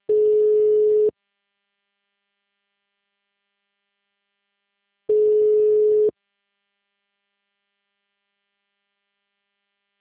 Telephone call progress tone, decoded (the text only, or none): ringback tone